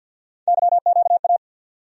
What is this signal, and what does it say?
Morse code, heard as a keyed tone, about 48 wpm, 700 Hz. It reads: YQA
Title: USA Call Sign Postfixes - 3 Letters 50wpm